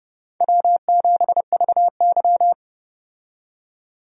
Morse code, heard as a keyed tone, about 30 wpm, 700 Hz. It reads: W7VY
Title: USA Call Signs - 4 Characters - 30wpm